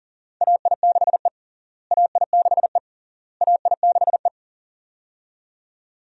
Morse code, heard as a keyed tone, about 37 wpm, 700 Hz. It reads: AI6E AI6E AI6E
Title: USA Call Signs - 4 Characters - Repeated 3x - 40wpm